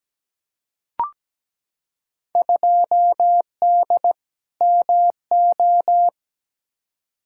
Morse code, heard as a keyed tone, about 17 wpm, 700 Hz. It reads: 2D MO
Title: Field Day Exchanges - 17wpm